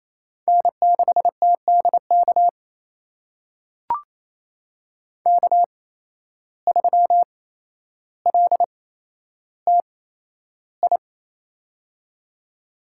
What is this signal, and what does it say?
Morse code, heard as a keyed tone, about 27 wpm, spaced as for 12 wpm, 700 Hz. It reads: N6TDK K3LTI
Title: USA Call Signs - 5 Characters - 8x Character Spacing - 28wpm